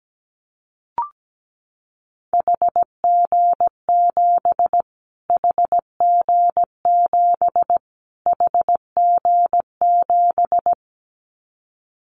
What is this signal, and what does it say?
Morse code, heard as a keyed tone, about 17 wpm, 700 Hz. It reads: HG7 HG7 HG7
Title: Callsign Prefixes for Top 25 DXCC - Repeated 3x - 17wpm